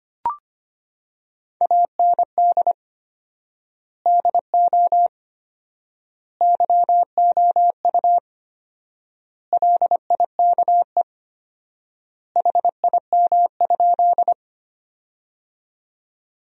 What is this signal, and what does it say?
Morse code, heard as a keyed tone, about 25 wpm, 700 Hz. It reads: AND DO YOU LIKE HIM?
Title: 4x Word-Spacing - Sentences from Top 100 Words - 25wpm